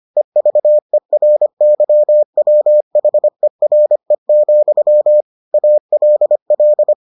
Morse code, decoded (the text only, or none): EVERYWHERE, ALL